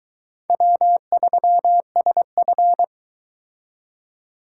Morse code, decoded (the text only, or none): W3SF